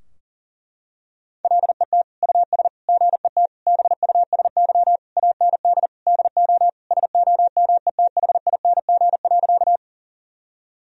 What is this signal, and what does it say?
Morse code, heard as a keyed tone, about 40 wpm, 700 Hz. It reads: LET US GET BUSY AND DO SOMETHING.
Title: Sentences from Top 1000 Words - Review Part 3 - 40wpm